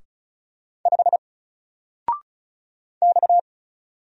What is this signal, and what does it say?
Morse code, heard as a keyed tone, about 35 wpm, 700 Hz. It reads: 5 X
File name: Single-Letter-Number-35wpm